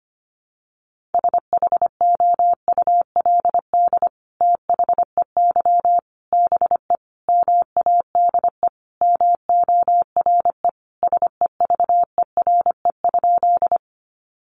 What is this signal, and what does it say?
Morse code, heard as a keyed tone, about 25 wpm, 700 Hz. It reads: SHOULD THEY BE MADE MORE SEVERE?
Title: Sentences from Top 1400 Words 25wpm